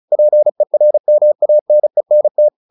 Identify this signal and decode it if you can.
Morse code, heard as a keyed tone, about 34 wpm, 600 Hz. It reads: PERMANENT
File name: the old man and the sea / the old man and the sea 1 35wpm